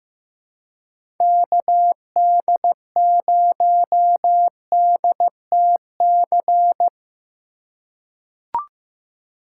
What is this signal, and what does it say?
Morse code, heard as a keyed tone, about 15 wpm, 700 Hz. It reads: KD0DTC
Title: USA Call Signs - 6 Characters - 3x Character Spacing - 15wpm